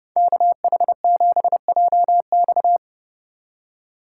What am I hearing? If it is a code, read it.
Morse code, heard as a keyed tone, about 29 wpm, 700 Hz. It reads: KH7JX